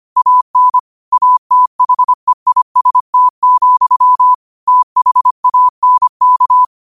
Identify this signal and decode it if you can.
Morse code, heard as a keyed tone, about 25 wpm, 1000 Hz. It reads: AN ATHEIST, THANK
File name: qotdfiles / 25wpm